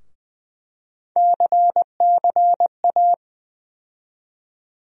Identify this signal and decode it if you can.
Morse code, heard as a keyed tone, about 20 wpm, 700 Hz. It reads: CCA
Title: USA Call Sign Postfixes - 3 Letters 20wpm